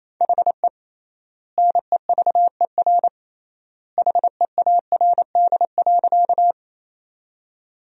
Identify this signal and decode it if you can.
Morse code, heard as a keyed tone, about 28 wpm, 700 Hz. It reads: HE NEVER HEARD.
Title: 3x Word-Spacing - Sentences from Top 400 Words - 28wpm